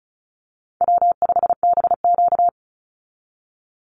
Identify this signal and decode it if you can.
Morse code, heard as a keyed tone, about 34 wpm, 700 Hz. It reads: W5BQ
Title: Easy Call Signs Repeated 2x - 35wpm